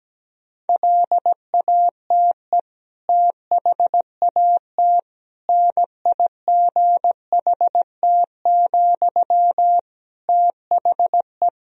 Morse code, heard as a keyed tone, about 17 wpm, 700 Hz. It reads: LATE THAT NIGHT, THE
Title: Sentences from Top 1000 Words - Review Part 4 - 17wpm